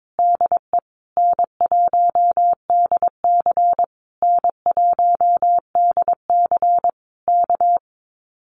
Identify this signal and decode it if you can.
Morse code, heard as a keyed tone, about 22 wpm, 700 Hz. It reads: DE N1DC N1DC K
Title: Common QSO Phrases Encore 22wpm